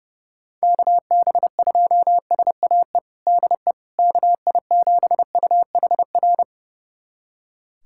Morse code, heard as a keyed tone, about 32 wpm, 700 Hz. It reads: KB2SAE DE KI7UHR
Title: USA Callsign DE Callsign - 6 Characters - 30wpm